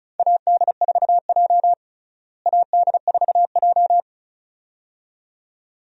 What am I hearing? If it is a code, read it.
Morse code, heard as a keyed tone, about 36 wpm, 700 Hz. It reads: AD4J AD4J
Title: USA Call Signs - 4 Characters - Repeated 2x - 35wpm